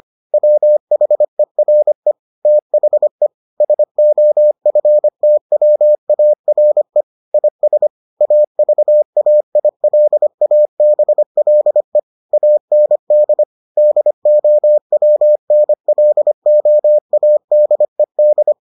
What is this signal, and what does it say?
Morse code, heard as a keyed tone, about 25 wpm, 600 Hz. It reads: WHERE THE SOFTWARE IS AVAILABLE AND DOWNLOADED